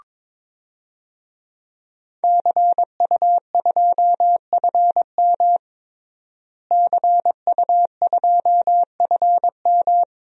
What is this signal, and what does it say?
Morse code, heard as a keyed tone, about 22 wpm, 700 Hz. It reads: CU2FM CU2FM
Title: DXCC-100 Call Signs - 5 Characters - Repeated 3x - 22wpm